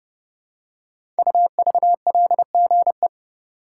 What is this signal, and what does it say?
Morse code, heard as a keyed tone, about 30 wpm, 700 Hz. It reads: UVLGE
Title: Code Groups - 5 Letter:Numbers - 30wpm